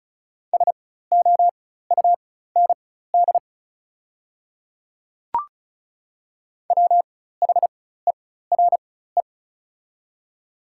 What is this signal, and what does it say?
Morse code, heard as a keyed tone, about 36 wpm, spaced as for 16 wpm, 690 Hz. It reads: SOUND WHERE
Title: 4x Character Spacing - Top 200 Words - 35wpm